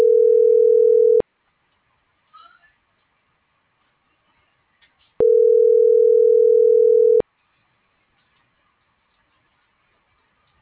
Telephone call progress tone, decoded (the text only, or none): ringback tone